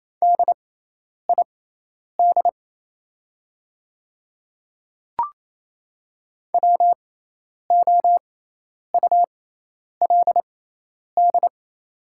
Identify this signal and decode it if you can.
Morse code, heard as a keyed tone, about 28 wpm, spaced as for 10 wpm, 690 Hz. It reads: DID WOULD
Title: 6x Character Spacing - Top 100 Words - 28wpm